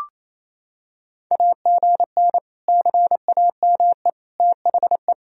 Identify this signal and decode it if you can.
Morse code, heard as a keyed tone, about 28 wpm, 700 Hz. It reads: AGN CAME THE
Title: Sentences from Top 500 Words and Abbreviations 28wpm